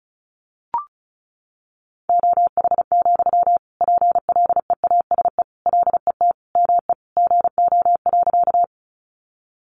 Morse code, heard as a keyed tone, about 34 wpm, 700 Hz. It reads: OH, PLEASE LET ME GO.
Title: Sentences from Top 1500 Words 35wpm